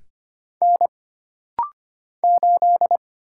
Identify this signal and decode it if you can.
Morse code, heard as a keyed tone, about 25 wpm, 700 Hz. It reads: N 8
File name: Single Character (Letter:Number:Punctuation:Prosign) - Rapid-Fire - 25wpm